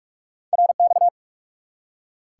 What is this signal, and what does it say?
Morse code, heard as a keyed tone, about 43 wpm, 700 Hz. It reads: RX
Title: Top 90 QSO Elements 45wpm